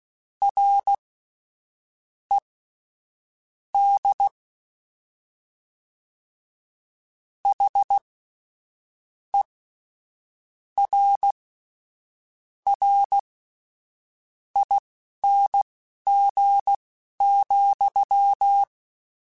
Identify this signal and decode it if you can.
Morse code, heard as a keyed tone, about 16 wpm, spaced as for 7 wpm, 800 Hz. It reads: RED HERRING,
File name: Misc / Misc-mp3s / WordPairs sp @10